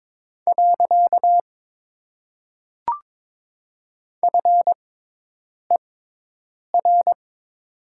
Morse code, heard as a keyed tone, about 22 wpm, spaced as for 8 wpm, 700 Hz. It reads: . FER
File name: Top 90 QSO Elements - 6x Character Spacing - 22wpm